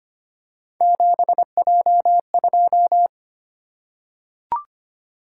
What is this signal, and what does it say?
Morse code, heard as a keyed tone, about 25 wpm, 700 Hz. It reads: 7J2
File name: Three Letter-Number Combinations - 25wpm